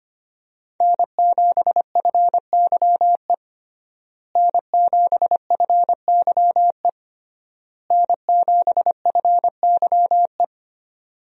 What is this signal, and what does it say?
Morse code, heard as a keyed tone, about 25 wpm, 700 Hz. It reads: N7FYE N7FYE N7FYE
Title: USA Call Signs - 5 Characters - Repeated 3x - 25wpm